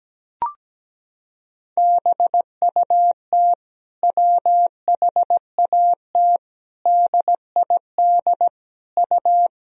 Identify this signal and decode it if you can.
Morse code, heard as a keyed tone, about 17 wpm, 700 Hz. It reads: BUT WHAT DID U
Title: Sentences from Top 100 Words and Abbreviations 17wpm